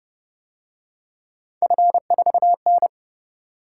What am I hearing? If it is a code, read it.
Morse code, heard as a keyed tone, about 31 wpm, 700 Hz. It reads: F4N
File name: Speed-Racing - Three Letter-Number Combinations - 25wpm